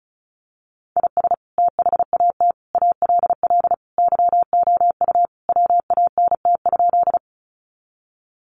Morse code, heard as a keyed tone, about 34 wpm, 700 Hz. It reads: IS THAT ALL YOU WANT?
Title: Sentences from Top 200 Words 35wpm